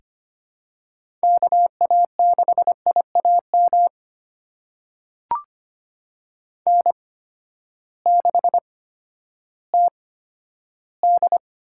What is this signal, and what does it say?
Morse code, heard as a keyed tone, about 25 wpm, spaced as for 12 wpm, 700 Hz. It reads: KA6IAM N6TD
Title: USA Call Signs - 8x Character Spacing - 25wpm